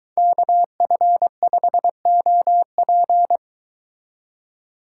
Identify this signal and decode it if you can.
Morse code, heard as a keyed tone, about 22 wpm, 700 Hz. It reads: KF5OP